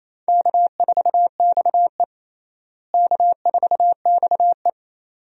Morse code, heard as a keyed tone, about 28 wpm, 700 Hz. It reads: K4XE K4XE
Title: USA Call Signs - 4 Characters - Repeated 2x - 28wpm